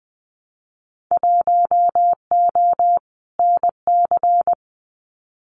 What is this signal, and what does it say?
Morse code, heard as a keyed tone, about 20 wpm, 700 Hz. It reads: 1O NC
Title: Winter Field Day Exchanges Encore 20wpm